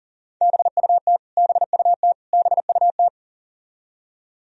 Morse code, heard as a keyed tone, about 40 wpm, 700 Hz. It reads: BUT BUT BUT
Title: Top 100 Words Repeated 40wpm